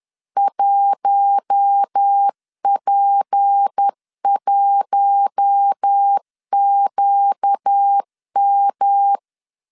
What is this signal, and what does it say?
Morse code, heard as a keyed tone, about 10 wpm, 800 Hz. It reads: 1P1QM
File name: LEZIONI / 6 PKX0 / 6 P 10wpm 2432